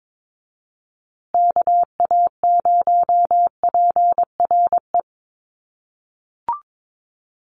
Morse code, heard as a keyed tone, about 22 wpm, 700 Hz. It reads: KA0PRE
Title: USA Call Signs Repeated 3x 22wpm